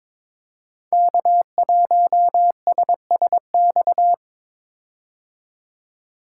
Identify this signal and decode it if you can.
Morse code, heard as a keyed tone, about 22 wpm, 700 Hz. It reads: K1SSX
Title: Code Groups - 5 Letter:Numbers - 22wpm